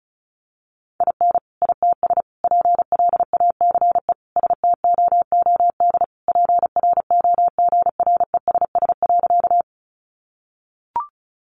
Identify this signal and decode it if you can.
Morse code, heard as a keyed tone, about 34 wpm, 700 Hz. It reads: IN ITS PLACE STOOD PROGRESS.
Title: Sentences from Top 1400 Words 35wpm